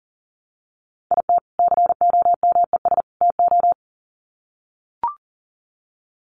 Morse code, heard as a keyed tone, about 40 wpm, 700 Hz. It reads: IT COMES TO
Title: Sets of 3 Words 40wpm